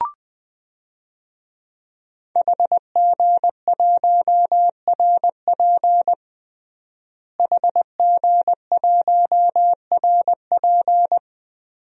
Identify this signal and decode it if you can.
Morse code, heard as a keyed tone, about 20 wpm, 700 Hz. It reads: HG1RP HG1RP
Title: DXCC-100 Call Signs - 5 Characters - Repeated 3x - 20wpm